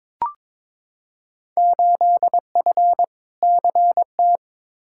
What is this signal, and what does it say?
Morse code, heard as a keyed tone, about 22 wpm, 700 Hz. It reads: 8F CT
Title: Field Day Exchanges Encore - 22wpm